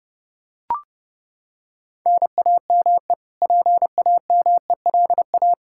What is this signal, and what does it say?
Morse code, heard as a keyed tone, about 30 wpm, 700 Hz. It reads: NAME PAMELA
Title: Practice QSO - Exchange 1 - Beginning - 30wpm